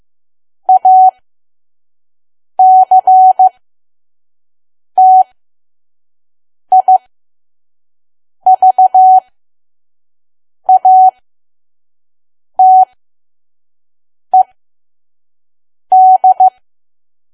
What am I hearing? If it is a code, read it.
Morse code, heard as a keyed tone, about 15 wpm, spaced as for 5 wpm, 750 Hz. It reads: ACTIVATED
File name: Archive / 5 WPM files / PM